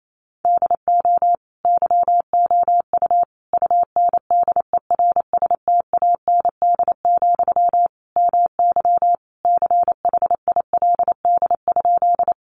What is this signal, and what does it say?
Morse code, heard as a keyed tone, about 28 wpm, 700 Hz. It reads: DO YOU UNDERSTAND, MY CHILD?
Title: Sentences from Top 800 Words 28wpm